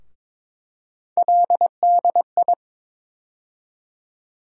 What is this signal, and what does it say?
Morse code, heard as a keyed tone, about 23 wpm, 700 Hz. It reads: LDI
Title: USA Call Sign Postfixes - 3 Letters 22wpm